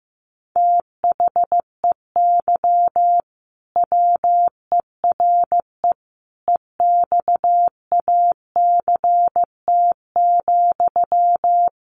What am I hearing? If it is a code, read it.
Morse code, heard as a keyed tone, about 15 wpm, 700 Hz. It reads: THEY WERE EXACT,
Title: Sentences from Top 1300 Words 15wpm